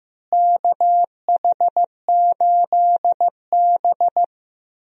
Morse code, heard as a keyed tone, about 15 wpm, 700 Hz. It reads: KH8B
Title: USA Call Signs - 4 Characters - Repeated 2x - 15wpm